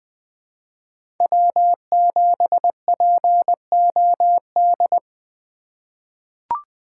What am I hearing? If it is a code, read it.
Morse code, heard as a keyed tone, about 20 wpm, 700 Hz. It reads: W7POD
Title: USA Call Signs - 5 Characters - 4x Character Spacing - 20wpm